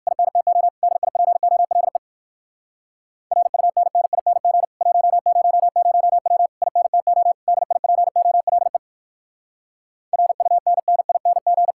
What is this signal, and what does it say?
Morse code, heard as a keyed tone, about 58 wpm, 700 Hz. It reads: INTO DIPOLE RUNNING 100W INTO DIPOLE RUNNING